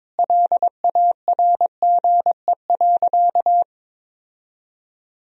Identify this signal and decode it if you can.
Morse code, heard as a keyed tone, about 22 wpm, 700 Hz. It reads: LARGE.
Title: Sentences from Top 1000 Words - Review Part 1 - 22wpm